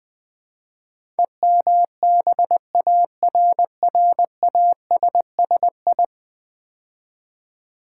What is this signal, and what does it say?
Morse code, heard as a keyed tone, about 20 wpm, 700 Hz. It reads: EMBARRASSI
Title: Build A Word - 5 Letters At A Time - 12-Letter Words - 20wpm